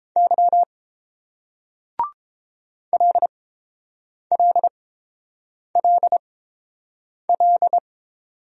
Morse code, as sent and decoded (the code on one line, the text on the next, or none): -.-- / .-.. .-.. .-.. .-..
Y LLLL